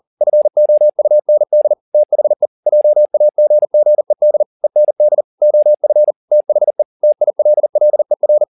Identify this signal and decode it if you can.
Morse code, heard as a keyed tone, about 40 wpm, 600 Hz. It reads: FOUND THE JAGGED END OF THE TILLER
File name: the old man and the sea 2 40wpm